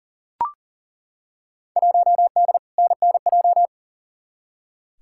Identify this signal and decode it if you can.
Morse code, heard as a keyed tone, about 40 wpm, 700 Hz. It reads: 1D NNJ
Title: Field Day Exchanges - 40wpm